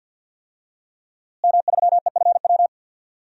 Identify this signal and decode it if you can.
Morse code, heard as a keyed tone, about 52 wpm, 700 Hz. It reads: M3EFW
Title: DXCC-100 Call Signs - 5 Characters - Repeated 2x - 50wpm